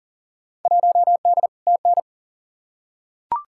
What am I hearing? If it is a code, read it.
Morse code, heard as a keyed tone, about 40 wpm, 700 Hz. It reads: 1D TN
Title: Field Day Exchanges - 40wpm